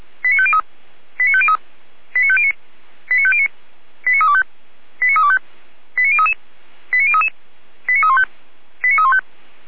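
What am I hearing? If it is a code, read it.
Five-tone selective call (ZVEI), heard as quick runs of five tones, tones of about 70 ms. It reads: 89503 89503 89509 89509 89325 89325 89033 89033 89315 89315
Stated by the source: ZVEI five-tone selective call 89121 received on an SDR.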